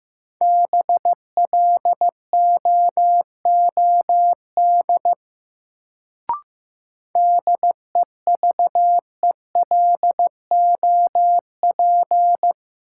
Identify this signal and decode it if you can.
Morse code, heard as a keyed tone, about 15 wpm, 700 Hz. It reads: BLOOD DEVELOP
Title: Sentences from Top 1000 Words - Review Part 2 - New Word Drill - 15wpm